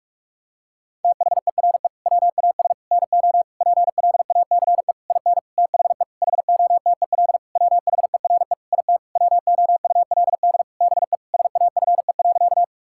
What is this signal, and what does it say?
Morse code, heard as a keyed tone, about 46 wpm, 700 Hz. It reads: THERE WAS NO PLACE IN THE HOTEL WHERE IT WOULD BE SAFE.
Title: Sentences from Top 1600 Words 45wpm